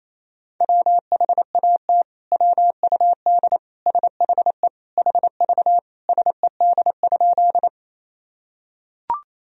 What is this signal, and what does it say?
Morse code, heard as a keyed tone, about 27 wpm, 700 Hz. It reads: WHAT WUD SHE HV SED?
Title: Sentences from Top 100 Words and Abbreviations 28wpm